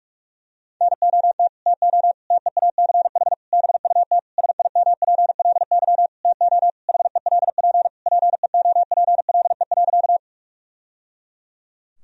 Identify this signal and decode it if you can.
Morse code, heard as a keyed tone, about 41 wpm, 700 Hz. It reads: NOT TO TEACH BUT SIMPLY TO HELP PEOPLE.
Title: Sentences from Top 1000 Words - Review Part 4 - 45wpm